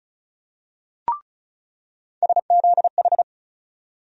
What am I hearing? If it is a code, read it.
Morse code, heard as a keyed tone, about 36 wpm, 700 Hz. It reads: SZH
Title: USA Call Sign Postfixes - 3 Letters 35wpm